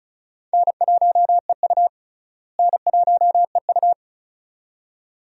Morse code, heard as a keyed tone, about 36 wpm, 700 Hz. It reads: N1EU N1EU
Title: USA Call Signs - 4 Characters - Repeated 2x - 35wpm